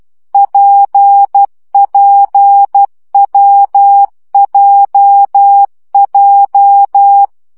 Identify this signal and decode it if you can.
Morse code, heard as a keyed tone, about 12 wpm, 800 Hz. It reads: PPWJJ
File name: CW gyakorlas w p j 10 baud